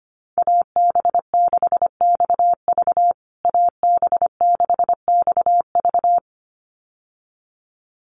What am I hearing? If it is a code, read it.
Morse code, heard as a keyed tone, about 25 wpm, 700 Hz. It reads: AB6XV AB6XV
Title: USA Call Signs Repeated 2x 25wpm